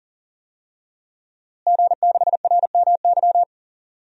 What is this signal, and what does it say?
Morse code, heard as a keyed tone, about 40 wpm, 700 Hz. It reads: G6RMY
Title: DXCC-100 Call Signs - 5 Characters - Repeated 2x - 40wpm